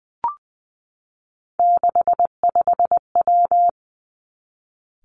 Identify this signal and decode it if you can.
Morse code, heard as a keyed tone, about 20 wpm, 700 Hz. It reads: 65W
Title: Three Challenging Letter-Number Combinations 20wpm